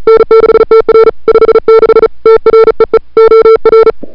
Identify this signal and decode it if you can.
Morse code, heard as a keyed tone, about 36 wpm, 440 Hz. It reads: N6TR 56 TREE OR
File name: CWSprint / February 2007 CW Sprint / 01-2007 CW Sprint Hour1